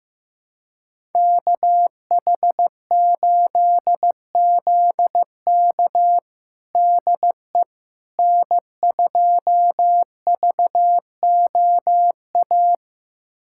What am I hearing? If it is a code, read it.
Morse code, heard as a keyed tone, about 15 wpm, 700 Hz. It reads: KH8ZK DE N2VOA
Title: USA Callsign DE Callsign - 5 Characters - 15wpm